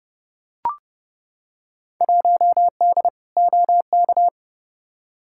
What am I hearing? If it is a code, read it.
Morse code, heard as a keyed tone, about 30 wpm, 700 Hz. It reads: 1D OK
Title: Field Day Exchanges - 30wpm